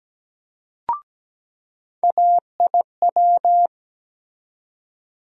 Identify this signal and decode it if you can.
Morse code, heard as a keyed tone, about 17 wpm, 700 Hz. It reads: AIW
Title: USA Call Sign Postfixes - 3 Letters 17wpm